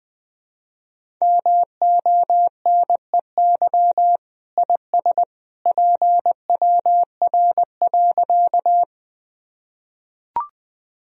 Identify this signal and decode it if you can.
Morse code, heard as a keyed tone, about 20 wpm, 700 Hz. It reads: MONEY IS PWR.
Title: Sentences from Top 500 Words and Abbreviations 20wpm